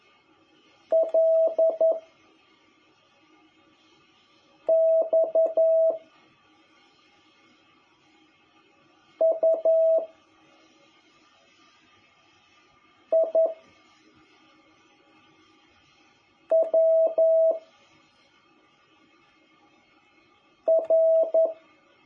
Morse code, heard as a keyed tone, about 11 wpm, 660 Hz. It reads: L X U I W R
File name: jh8cbh / 2021cw